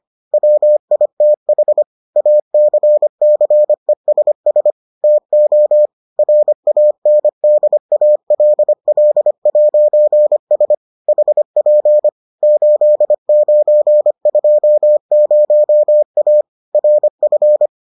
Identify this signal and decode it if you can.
Morse code, heard as a keyed tone, about 25 wpm, 600 Hz. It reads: WITH ACCESS TO RANDALL'S HP 8920A RF